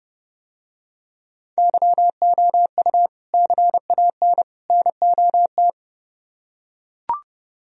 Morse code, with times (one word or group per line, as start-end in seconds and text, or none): 1.6-3.1 YOU
3.3-4.4 CAN
4.7-5.7 NOT